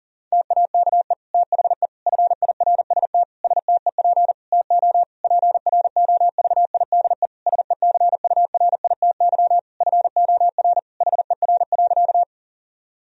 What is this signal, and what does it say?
Morse code, heard as a keyed tone, about 40 wpm, 700 Hz. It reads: TAKE THE FIRST STEP TO PROVIDE SECURITY FOR HER.
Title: Sentences from Top 1000 Words - Review Part 4 - 40wpm